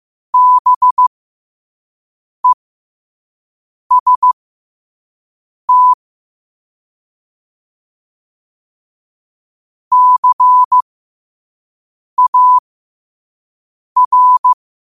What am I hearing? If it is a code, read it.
Morse code, heard as a keyed tone, about 15 wpm, spaced as for 5 wpm, 1000 Hz. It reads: BEST CAR
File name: qotdfiles / 5wpm